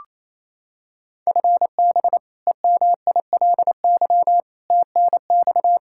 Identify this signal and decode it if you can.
Morse code, heard as a keyed tone, about 28 wpm, 700 Hz. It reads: FB EMILY TNX
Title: Practice QSO - Exchange 3 - More Info - 28wpm